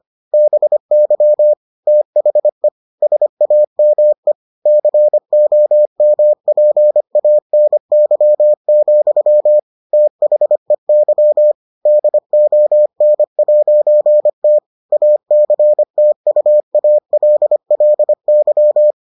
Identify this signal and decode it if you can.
Morse code, heard as a keyed tone, about 25 wpm, 600 Hz. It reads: BY THE SAME COMPANY, THEY DON'T ACTUALLY